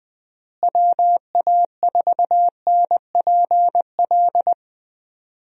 Morse code, heard as a keyed tone, about 19 wpm, 700 Hz. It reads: WA4NPL
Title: USA Call Signs - 6 Characters - Repeated 3x - 20wpm